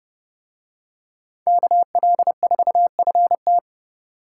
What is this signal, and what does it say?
Morse code, heard as a keyed tone, about 30 wpm, 700 Hz. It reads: KL4FT